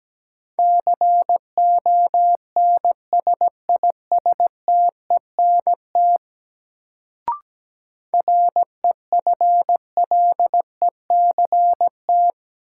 Morse code, heard as a keyed tone, about 17 wpm, 700 Hz. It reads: CONSISTENT REFLECT
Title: Sentences from Top 1500 Words - New Word Drill - 17wpm